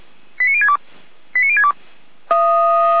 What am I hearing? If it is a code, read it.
Five-tone selective call (ZVEI), heard as quick runs of five tones, tones of about 80 ms. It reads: 89062 89062
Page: /455/SDR/ZVEI/89115